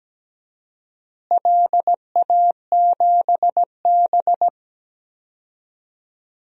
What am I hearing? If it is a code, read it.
Morse code, heard as a keyed tone, about 17 wpm, 700 Hz. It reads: LA7B
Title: DXCC-100 Call Signs - 4 Characters - Repeated 2x - 17wpm